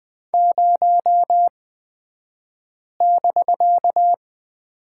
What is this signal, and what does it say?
Morse code, heard as a keyed tone, about 20 wpm, spaced as for 5 wpm, 700 Hz. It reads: 0<BK>
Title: 249 Lesson 40d - BK - Two Characters 20wpm